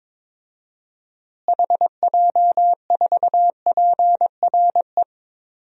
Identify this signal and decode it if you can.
Morse code, heard as a keyed tone, about 22 wpm, 700 Hz. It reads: HJ4PRE
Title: DXCC-100 Call Signs - 6 Characters - Repeated 2x - 22wpm